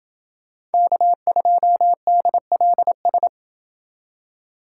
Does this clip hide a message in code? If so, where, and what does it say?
Yes: Morse code from 0.7 to 3.3 s, heard as a keyed tone, about 29 wpm, 700 Hz. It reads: K2DLS